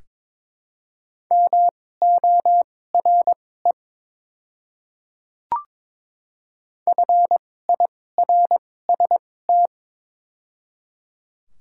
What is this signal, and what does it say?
Morse code, heard as a keyed tone, about 22 wpm, spaced as for 16 wpm, 700 Hz. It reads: MORE FIRST
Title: 2x Character Spacing - Top 200 Words - 22wpm